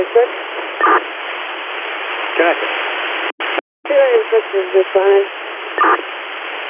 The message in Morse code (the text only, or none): none